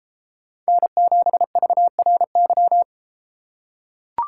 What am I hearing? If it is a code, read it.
Morse code, heard as a keyed tone, about 33 wpm, 700 Hz. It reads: N7VRY